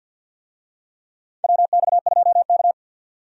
Morse code, heard as a keyed tone, about 51 wpm, 700 Hz. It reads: WX2K
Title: USA Call Signs - 4 Characters - 50wpm